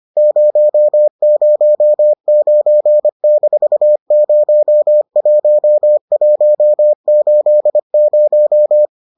Morse code, heard as a keyed tone, about 25 wpm, 600 Hz. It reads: 009-01180